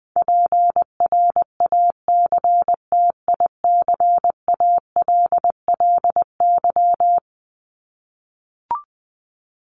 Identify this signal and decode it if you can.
Morse code, heard as a keyed tone, about 20 wpm, 700 Hz. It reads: PRACTICALLY
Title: LY Suffix - Top 100 Words with Base Words 20wpm